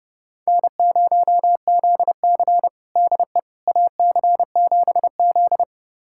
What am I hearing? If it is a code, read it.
Morse code, heard as a keyed tone, about 30 wpm, 700 Hz. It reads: N0ZC DE AC7Z
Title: USA Callsign DE Callsign - 4 Characters - 30wpm